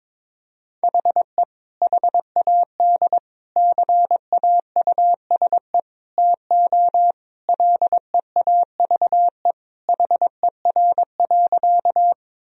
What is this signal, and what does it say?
Morse code, heard as a keyed tone, about 22 wpm, 700 Hz. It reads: HE HAD CAUSE TO LEAVE HER.
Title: Sentences from Top 1000 Words - Review Part 1 - 22wpm